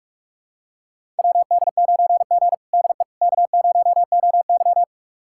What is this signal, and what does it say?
Morse code, heard as a keyed tone, about 43 wpm, 700 Hz. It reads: WD9G DE K0OY
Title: USA Callsign DE Callsign - 4 Characters - 45wpm